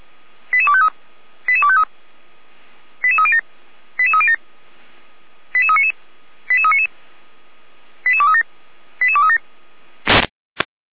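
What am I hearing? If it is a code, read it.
Five-tone selective call (ZVEI), heard as quick runs of five tones, tones of about 70 ms. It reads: 88253 88253 88307 88307 88399 88399 88237 88237
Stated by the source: ZVEI five-tone selective call received on SDR.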